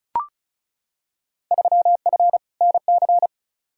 Morse code, heard as a keyed tone, about 36 wpm, 690 Hz. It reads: 3F NC
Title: Field Day Exchanges Encore - 35wpm